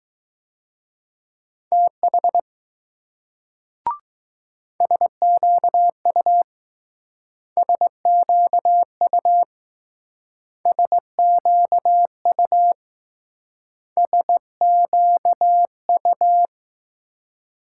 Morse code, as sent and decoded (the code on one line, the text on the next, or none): - .... / ... --.- ..- / ... --.- ..- / ... --.- ..- / ... --.- ..-
TH SQU SQU SQU SQU